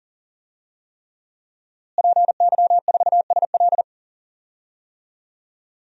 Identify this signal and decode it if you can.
Morse code, heard as a keyed tone, about 37 wpm, 700 Hz. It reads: PY4SL
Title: DXCC-100 Call Signs - 5 Characters - Repeated 3x - 40wpm